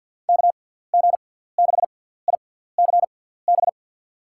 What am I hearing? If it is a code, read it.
Morse code, heard as a keyed tone, about 51 wpm, spaced as for 17 wpm, 700 Hz. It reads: KG6ICB
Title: USA Call Signs - 6x Character Spacing - 50wpm